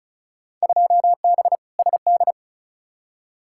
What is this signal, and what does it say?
Morse code, heard as a keyed tone, about 36 wpm, 700 Hz. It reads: 2B SD
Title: Field Day Exchanges Encore - 35wpm